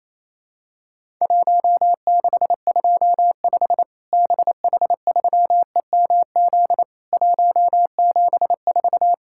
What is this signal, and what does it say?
Morse code, heard as a keyed tone, about 28 wpm, 700 Hz. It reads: 1625 BH3EMZ 174
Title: Worked All Europe DX Contest - QTC Exchange - 28wpm